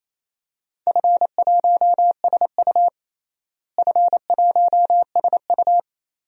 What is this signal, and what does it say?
Morse code, heard as a keyed tone, about 28 wpm, 700 Hz. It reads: F1SU F1SU
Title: DXCC-100 Call Signs - 4 Characters - Repeated 2x - 28wpm